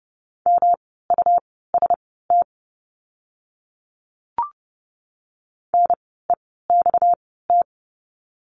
Morse code, heard as a keyed tone, about 30 wpm, spaced as for 17 wpm, 700 Hz. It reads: MUST NEXT
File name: course / 30:10wpm / 237 Lesson 38f - Letter q - Words 30wpm